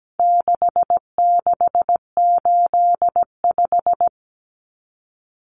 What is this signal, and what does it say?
Morse code, heard as a keyed tone, about 17 wpm, 700 Hz. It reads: 6685
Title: Common Cut Numbers - 4-Digit 17wpm